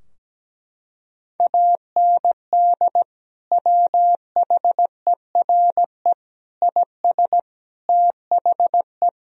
Morse code, heard as a keyed tone, about 17 wpm, 700 Hz. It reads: AND WHERE IS THE
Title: Sentences from Top 1000 Words - Review Part 3 - 17wpm